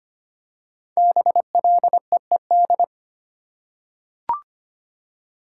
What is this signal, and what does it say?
Morse code, heard as a keyed tone, about 25 wpm, 700 Hz. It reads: BLEED
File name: 5 Letter Words - First 3 Letters Given - 25wpm